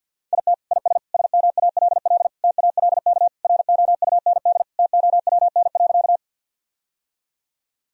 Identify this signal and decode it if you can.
Morse code, heard as a keyed tone, about 51 wpm, 700 Hz. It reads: IT IS SMALL TALK ROUND TOWN.
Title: Sentences from Top 1000 Words - Review Part 1 - 50wpm